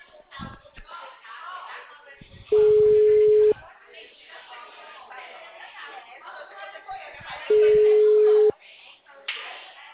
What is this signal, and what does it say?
Ringback tone (Europe, 425 Hz), with speech.